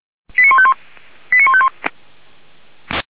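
Five-tone selective call (ZVEI), heard as quick runs of five tones, tones of about 70 ms. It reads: 89151 89151
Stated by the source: ZVEI five-tone selective call received by SDR.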